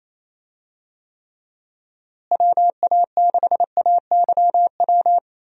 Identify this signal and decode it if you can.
Morse code, heard as a keyed tone, about 28 wpm, 700 Hz. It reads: WA6AYW
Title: USA Call Signs - 6 Characters - Repeated 2x - 28wpm